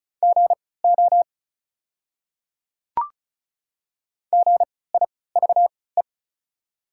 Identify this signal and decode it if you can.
Morse code, heard as a keyed tone, about 34 wpm, spaced as for 20 wpm, 700 Hz. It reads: GO GIVE
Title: 3x Character Spacing - Top 200 Words - 35wpm